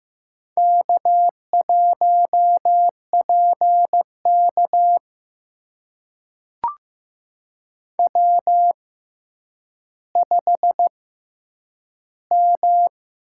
Morse code, heard as a keyed tone, about 15 wpm, spaced as for 9 wpm, 700 Hz. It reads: K1PK W5M
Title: USA Call Signs - 4 Characters - 6x Character Spacing - 15wpm